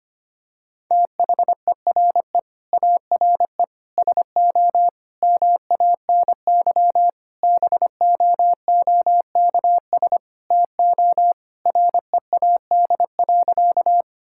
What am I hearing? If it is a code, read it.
Morse code, heard as a keyed tone, about 25 wpm, 700 Hz. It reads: THERE ARE SO MANY BOOKS TO READ.
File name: practice / 25wpm / Sentences with 4-Word Phrases - 25wpm